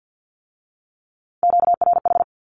Morse code, heard as a keyed tone, about 50 wpm, 700 Hz. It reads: QRH
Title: Q-Codes with Spoken Meaning - 50wpm